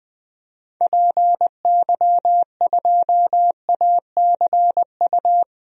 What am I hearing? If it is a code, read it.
Morse code, heard as a keyed tone, about 20 wpm, 700 Hz. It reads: PY2ACU
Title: DXCC-100 Call Signs - 6 Characters - Repeated 3x - 20wpm